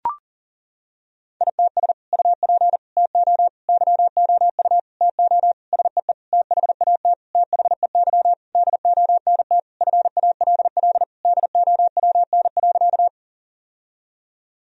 Morse code, heard as a keyed tone, about 40 wpm, 700 Hz. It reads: ITS UP TO YOU TO SEE THAT THEY DONT FALL DOWN.
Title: Sentences from Top 1000 Words - Review Part 2 - 40wpm